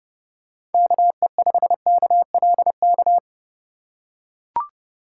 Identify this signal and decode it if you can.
Morse code, heard as a keyed tone, about 31 wpm, 700 Hz. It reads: KE5KLK